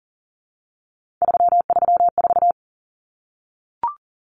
Morse code, heard as a keyed tone, about 40 wpm, 700 Hz. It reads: 334
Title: Common Cut Numbers - 3-Digit 40wpm